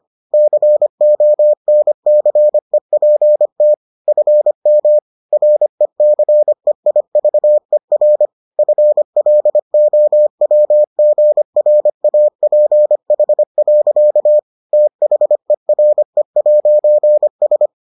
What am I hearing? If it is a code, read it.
Morse code, heard as a keyed tone, about 25 wpm, 600 Hz. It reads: CONCEPT FM RECEIVER FLOWGRAPH. THERE'S